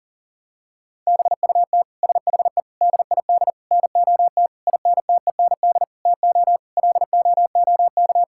Morse code, heard as a keyed tone, about 37 wpm, 700 Hz. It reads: BUT SHE DID NOT INTEND TO LOOK
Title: Sentences from Top 1600 Words 40wpm